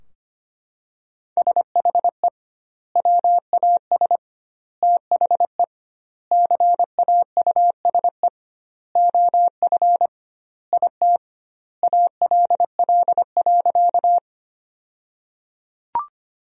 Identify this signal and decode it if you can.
Morse code, heard as a keyed tone, about 25 wpm, 700 Hz. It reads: SHE WAS THE CAUSE OF IT ALL.
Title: 2x Word-Spacing - Sentences from Top 200 Words - 25wpm